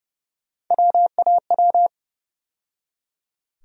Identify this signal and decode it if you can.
Morse code, heard as a keyed tone, about 29 wpm, 700 Hz. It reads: WAW
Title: USA Call Sign Postfixes - 3 Letters 30wpm